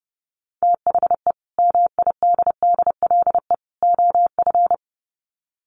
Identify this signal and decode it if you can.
Morse code, heard as a keyed tone, about 28 wpm, 700 Hz. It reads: THE MIDDLE OF
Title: Sets of 3 Words 30wpm